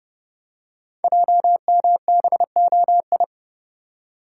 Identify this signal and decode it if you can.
Morse code, heard as a keyed tone, about 29 wpm, 700 Hz. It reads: JMBOI
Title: Code Groups - 5 Letters - 30wpm